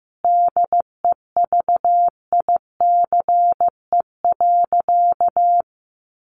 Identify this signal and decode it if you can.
Morse code, heard as a keyed tone, about 15 wpm, 700 Hz. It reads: DEVICE.
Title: Sentences from Top 1500 Words 15wpm